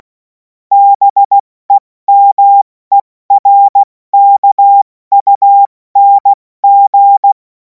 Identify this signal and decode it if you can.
Morse code, heard as a keyed tone, about 16 wpm, spaced as for 14 wpm, 800 Hz. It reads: BEMERKUNG
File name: Morsekurs / Witze / Witz 03. Februar 2026 70 F80